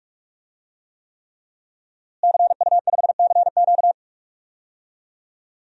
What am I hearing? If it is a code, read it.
Morse code, heard as a keyed tone, about 46 wpm, 700 Hz. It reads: CU5CQ
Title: DXCC-100 Call Signs - 5 Characters - Repeated 3x - 45wpm